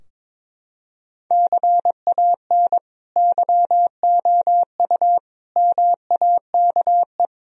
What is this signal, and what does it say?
Morse code, heard as a keyed tone, about 22 wpm, 700 Hz. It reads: CAN YOU MAKE
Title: Sentences from Top 1000 Words - Review Part 3 - 22wpm